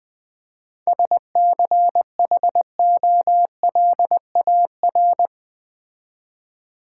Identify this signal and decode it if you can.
Morse code, heard as a keyed tone, about 20 wpm, 700 Hz. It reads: SCHOLAR
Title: 7 Letter Words - First 3 Letters Given - 20wpm